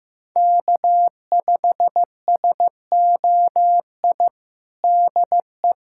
Morse code, heard as a keyed tone, about 15 wpm, 700 Hz. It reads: K5SOI DE
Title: USA Callsign DE Callsign 15wpm